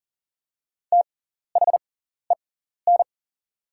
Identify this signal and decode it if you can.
Morse code, heard as a keyed tone, about 39 wpm, spaced as for 14 wpm, 700 Hz. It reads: THEN
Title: 6x Character Spacing - Top 200 Words - 40wpm